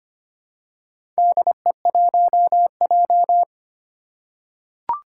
Morse code, heard as a keyed tone, about 25 wpm, 700 Hz. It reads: DE1J
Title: Four Challenging Letter-Number Combinations 25wpm